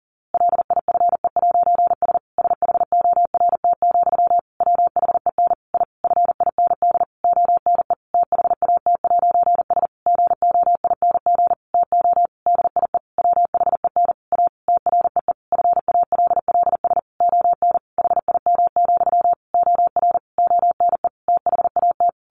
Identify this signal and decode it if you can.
Morse code, heard as a keyed tone, about 40 wpm, 700 Hz. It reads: LIFE'S SHORT, WHEN I FIND ONE THAT'S GOING TO DIE WHEN A TREE FALLS ON HIM, OR ONE THAT
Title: A Sound Of Thunder 40wpm